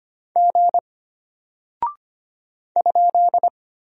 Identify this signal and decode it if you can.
Morse code, heard as a keyed tone, about 25 wpm, 700 Hz. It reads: G ?
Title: Single Character (Letter:Number:Punctuation:Prosign) Alt 25wpm